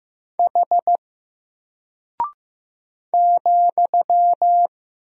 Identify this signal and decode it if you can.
Morse code, heard as a keyed tone, about 15 wpm, 700 Hz. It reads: H ,
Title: Single Character (Letter:Number:Punctuation:Prosign) Alt 15wpm